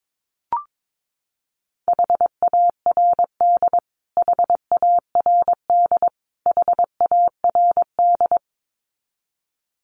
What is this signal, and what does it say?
Morse code, heard as a keyed tone, about 22 wpm, 700 Hz. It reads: HARD HARD HARD
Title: Top 500 Words Repeated 22wpm